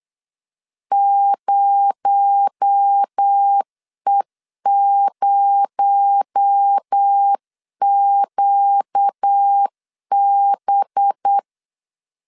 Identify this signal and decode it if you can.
Morse code, heard as a keyed tone, about 8 wpm, 800 Hz. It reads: 0E0QB
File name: CW / DET / 1 6 1500C S C 8wpm 5219